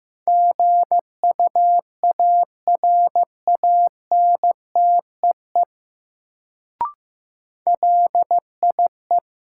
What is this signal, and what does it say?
Morse code, heard as a keyed tone, about 15 wpm, 700 Hz. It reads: GUARANTEE LIE
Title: Sentences from Top 1600 Words - New Word Drill Alt - 15wpm